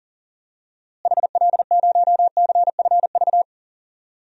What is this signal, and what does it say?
Morse code, heard as a keyed tone, about 37 wpm, 700 Hz. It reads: HL0CFV
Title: DXCC-100 Call Signs - 6 Characters - Repeated 2x - 40wpm